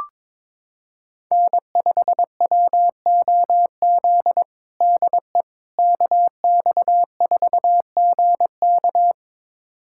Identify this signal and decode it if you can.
Morse code, heard as a keyed tone, about 22 wpm, 700 Hz. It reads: N5WOZ DE KX4GK
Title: USA Callsign DE Callsign - 5 Characters - 22wpm